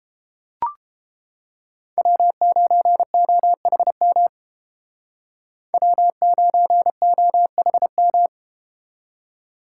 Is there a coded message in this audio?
Yes: Morse code from 2.0 to 8.3 s, heard as a keyed tone, about 31 wpm, 700 Hz. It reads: W9OHM W9OHM